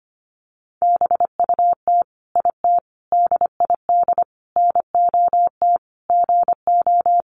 Morse code, heard as a keyed tone, about 25 wpm, 700 Hz. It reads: BUT IT DID NOT GO
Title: Sentences from Top 600 Words 25wpm